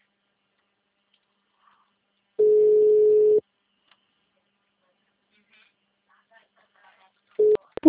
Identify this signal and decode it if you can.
Ringback tone (Europe, 425 Hz).
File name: out- -2008- - - .84096